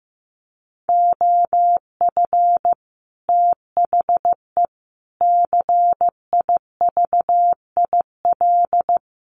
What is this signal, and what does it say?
Morse code, heard as a keyed tone, about 15 wpm, 700 Hz. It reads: OF THE CIVIL
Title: Sets of 5 Words Encore2 15wpm